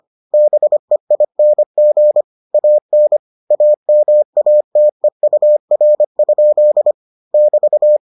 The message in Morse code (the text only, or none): BEING AN AMATEUR? <BT>